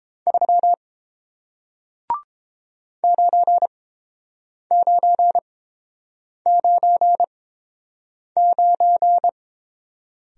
Morse code, heard as a keyed tone, about 31 wpm, spaced as for 7 wpm, 700 Hz. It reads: 3 9999